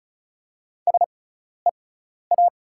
Morse code, heard as a keyed tone, about 36 wpm, spaced as for 12 wpm, 690 Hz. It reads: SEA
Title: 6x Character Spacing - Top 500 Words - 35wpm